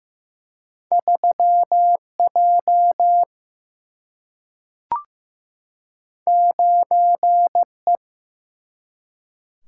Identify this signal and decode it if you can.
Morse code, heard as a keyed tone, about 15 wpm, 700 Hz. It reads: 3J 9E
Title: Single-Letter Single-Number 15wpm